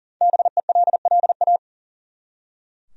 Morse code, heard as a keyed tone, about 40 wpm, 700 Hz. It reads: BELLA
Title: USA Common Names - Max 5 Letters - 40wpm